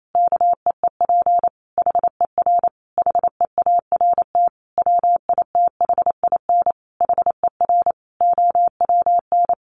Morse code, heard as a keyed tone, about 28 wpm, 700 Hz. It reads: KEEP HER HEART WITHIN HER OWN
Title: Sentences from Top 1200 Words 28wpm